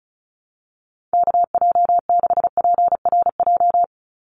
Morse code, heard as a keyed tone, about 34 wpm, 700 Hz. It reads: KJ6PRJ
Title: USA Call Signs 35wpm